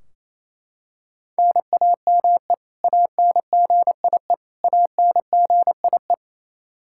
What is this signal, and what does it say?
Morse code, heard as a keyed tone, about 28 wpm, 700 Hz. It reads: NAME ANGIE ANGIE
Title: Practice QSO - Exchange 1 - Beginning - 28wpm